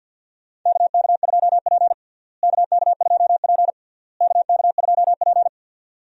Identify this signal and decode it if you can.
Morse code, heard as a keyed tone, about 47 wpm, 700 Hz. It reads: KK2P KK2P KK2P
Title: USA Call Signs - 4 Characters - Repeated 3x - 50wpm